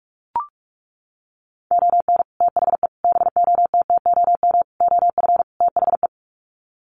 Morse code, heard as a keyed tone, about 44 wpm, 700 Hz. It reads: ON THE BOTTOM OF THE
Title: Sets of 5 Words Encore2 45wpm